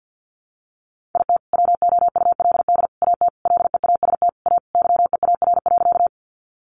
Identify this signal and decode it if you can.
Morse code, heard as a keyed tone, about 50 wpm, 700 Hz. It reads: IT WOULD AT LEAST A YEAR.
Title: Sentences from Top 700 Words 50wpm